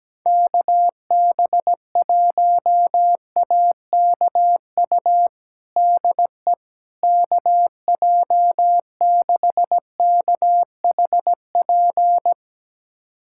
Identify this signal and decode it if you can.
Morse code, heard as a keyed tone, about 17 wpm, 700 Hz. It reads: KB1AKU DE KJ6KHP
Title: USA Callsign DE Callsign - 6 Characters - 17wpm